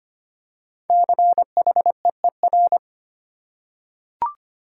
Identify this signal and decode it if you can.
Morse code, heard as a keyed tone, about 25 wpm, 700 Hz. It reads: CHEER
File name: 5 Letter Words - First 3 Letters Given - 25wpm